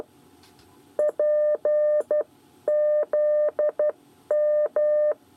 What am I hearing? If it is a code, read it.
Morse code, heard as a keyed tone, about 12 wpm, spaced as for 10 wpm, 600 Hz. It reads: PZM
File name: 8 bis lezione cw 1